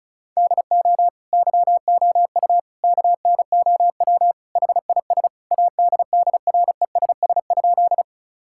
Morse code, heard as a keyed tone, about 36 wpm, 700 Hz. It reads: DO YOU KNOW HIS ADDRESS?
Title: Sentences from Top 1000 Words - Review Part 6 - 35wpm